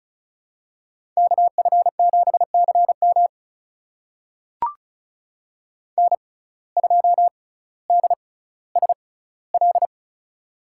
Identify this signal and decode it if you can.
Morse code, heard as a keyed tone, about 36 wpm, spaced as for 18 wpm, 700 Hz. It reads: KF7CM N2DSL
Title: USA Call Signs - 5 Characters - 6x Character Spacing - 35wpm